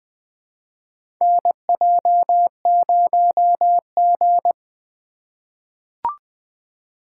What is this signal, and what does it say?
Morse code, heard as a keyed tone, about 20 wpm, 700 Hz. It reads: NJ0G
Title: USA Call Signs - 4 Characters - 2x Character Spacing - 20wpm